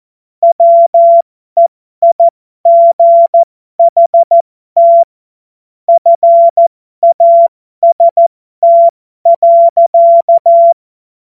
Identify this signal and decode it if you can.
Morse code, heard as a keyed tone, about 14 wpm, spaced as for 12 wpm, 680 Hz. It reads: WEIGHT FAST.
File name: Morsekurs / Witze